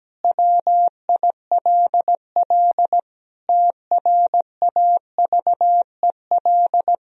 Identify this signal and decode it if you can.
Morse code, heard as a keyed tone, about 17 wpm, 700 Hz. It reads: WILL TRAVEL
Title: Sentences from Top 1000 Words - Review Part 2 - 17wpm